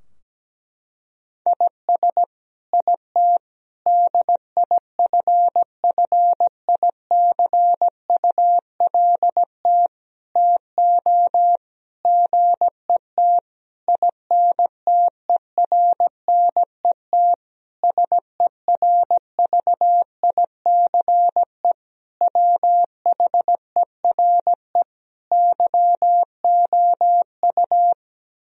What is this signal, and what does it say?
Morse code, heard as a keyed tone, about 17 wpm, 700 Hz. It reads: IS IT DIFFICULT TO GET INTERNET SERVICE WHERE YOU
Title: Sentences from Top 1000 Words - Review Part 4 - 17wpm